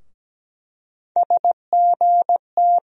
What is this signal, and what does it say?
Morse code, heard as a keyed tone, about 17 wpm, 700 Hz. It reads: SGT
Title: USA Call Sign Postfixes - 3 Letters 17wpm